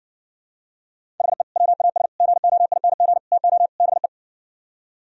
Morse code, heard as a keyed tone, about 69 wpm, 700 Hz. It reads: HE WAS GOING TO BE